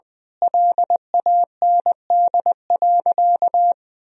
Morse code, heard as a keyed tone, about 21 wpm, 700 Hz. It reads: LAND.
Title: Sentences from Top 1000 Words - Review Part 2 - 20wpm